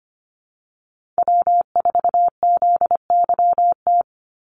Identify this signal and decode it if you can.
Morse code, heard as a keyed tone, about 25 wpm, 700 Hz. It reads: W4ZYT
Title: Call Signs - 25wpm